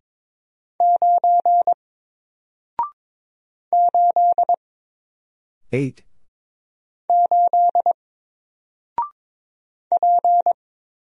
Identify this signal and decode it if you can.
Morse code, heard as a keyed tone, about 22 wpm, 700 Hz. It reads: 9 8 8 P
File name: Single-Letter-Number-22wpm